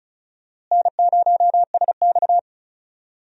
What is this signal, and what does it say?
Morse code, heard as a keyed tone, about 36 wpm, 700 Hz. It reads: N0SX
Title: USA Call Signs - 4 Characters - Repeated 3x - 35wpm